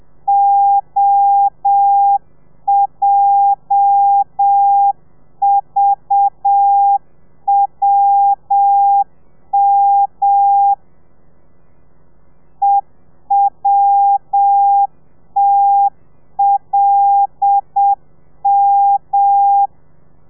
Morse code, heard as a keyed tone, about 7 wpm, 800 Hz. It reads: OJVWM EWTLM
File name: telegrafia / i2tfj / Lez8 35